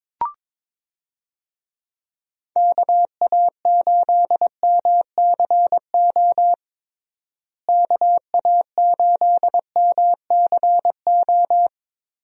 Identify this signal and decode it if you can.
Morse code, heard as a keyed tone, about 22 wpm, 700 Hz. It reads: KA8MCO KA8MCO
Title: USA Call Signs - 6 Characters - Repeated 3x - 22wpm